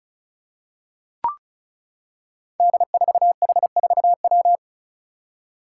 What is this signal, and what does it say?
Morse code, heard as a keyed tone, about 34 wpm, 700 Hz. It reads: D4H4W
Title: Five Challenging Letter-Number Combinations 35wpm